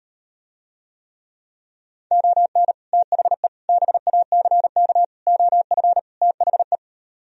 Morse code, heard as a keyed tone, about 38 wpm, 700 Hz. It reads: ON THE BACK OF THE